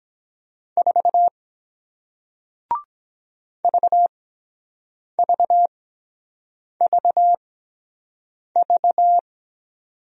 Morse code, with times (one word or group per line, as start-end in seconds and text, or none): 0.8-1.3 4
3.6-9.2 VVVV